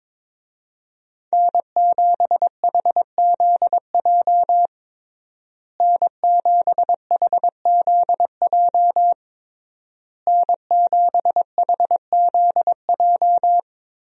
Morse code, heard as a keyed tone, about 22 wpm, 700 Hz. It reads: N7HZJ N7HZJ N7HZJ
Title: USA Call Signs - 5 Characters - Repeated 3x - 22wpm